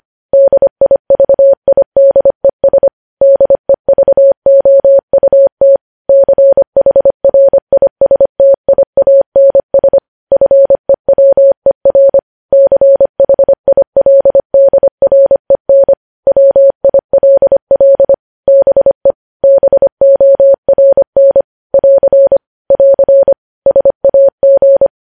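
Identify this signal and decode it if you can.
Morse code, heard as a keyed tone, about 25 wpm, 550 Hz. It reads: DIVIDES DEVOUT CHRISTIANS FEWER CHILDREN WILL BE BORN <AR> <AR> SAG
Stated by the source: Morse code news headlines sent at 25 wpm.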